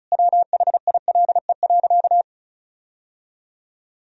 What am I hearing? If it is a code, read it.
Morse code, heard as a keyed tone, about 34 wpm, 700 Hz. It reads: WHILE.
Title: Sentences from Top 1000 Words - Review Part 1 - 35wpm